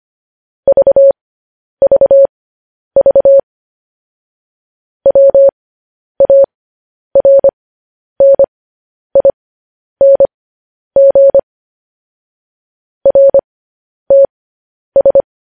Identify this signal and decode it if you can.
Morse code, heard as a keyed tone, about 25 wpm, spaced as for 10 wpm, 550 Hz. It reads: VVV WARNING RTS
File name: podcast / News headlines-1x-25x10